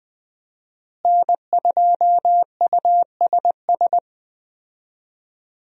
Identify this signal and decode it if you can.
Morse code, heard as a keyed tone, about 20 wpm, 700 Hz. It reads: N2USS
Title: USA Call Signs - 5 Characters - Repeated 2x - 20wpm